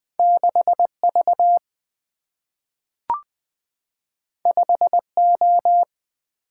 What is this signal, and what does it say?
Morse code, heard as a keyed tone, about 20 wpm, 700 Hz. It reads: 6V 5O
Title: Single-Letter Single-Number 20wpm